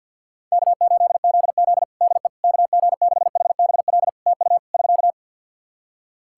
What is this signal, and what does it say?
Morse code, heard as a keyed tone, about 50 wpm, 700 Hz. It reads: K8ZZ DE KG6HBL TU <SK>
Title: Practice QSO - Exchange 4 - Ending - 50wpm